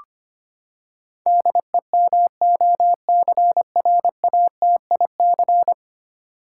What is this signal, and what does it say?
Morse code, heard as a keyed tone, about 25 wpm, 700 Hz. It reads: DEMOCRATIC
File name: Ten Letter Words - 25wpm